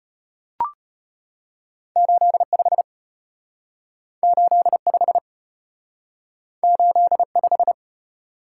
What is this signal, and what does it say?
Morse code, heard as a keyed tone, about 33 wpm, 700 Hz. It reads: 8 5 8 5 8 5